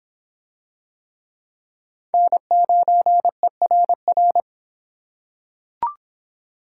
Morse code, heard as a keyed tone, about 24 wpm, 700 Hz. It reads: N9ERR